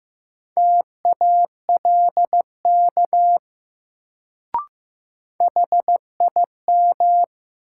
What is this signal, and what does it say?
Morse code, heard as a keyed tone, about 15 wpm, 700 Hz. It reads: TALK HIM
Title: Sets of 2 Words - New Word Drill - 15wpm